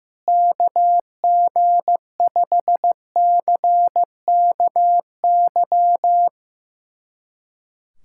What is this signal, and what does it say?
Morse code, heard as a keyed tone, about 15 wpm, 700 Hz. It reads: KG5CKY
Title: USA Callsign DE Callsign - 6 Characters - 15wpm